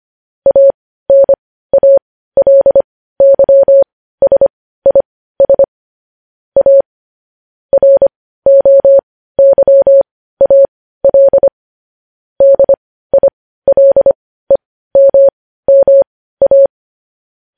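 Morse code, heard as a keyed tone, about 25 wpm, spaced as for 15 wpm, 550 Hz. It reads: ANALYSIS A ROYAL DILEMMA
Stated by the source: Morse code news headlines at 15 wpm.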